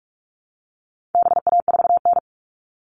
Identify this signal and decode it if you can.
Morse code, heard as a keyed tone, about 44 wpm, 700 Hz. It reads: BA4N
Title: All Cut Numbers - 4-Digit 45wpm